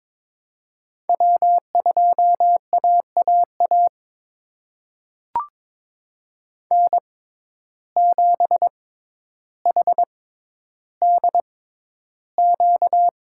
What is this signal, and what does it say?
Morse code, heard as a keyed tone, about 22 wpm, spaced as for 11 wpm, 700 Hz. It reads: W2AAA N7HDQ
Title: USA Call Signs - 5 Characters - 6x Character Spacing - 22wpm